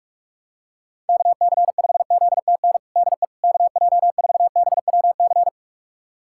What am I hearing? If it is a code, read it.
Morse code, heard as a keyed tone, about 46 wpm, 700 Hz. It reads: KC5ZTN DE KJ4BWC
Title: USA Callsign DE Callsign 45wpm